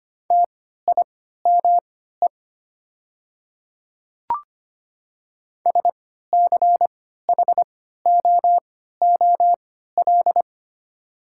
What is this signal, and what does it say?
Morse code, heard as a keyed tone, about 25 wpm, spaced as for 14 wpm, 690 Hz. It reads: TIME SCHOOL
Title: 3x Character Spacing - Top 300 Words - 25wpm